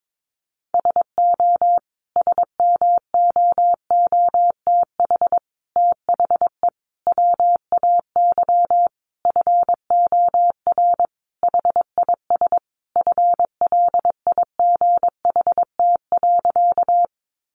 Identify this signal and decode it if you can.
Morse code, heard as a keyed tone, about 22 wpm, 700 Hz. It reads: SO SMOOTH THE WAY FOR HIS FLIGHT.
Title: Sentences from Top 1400 Words 22wpm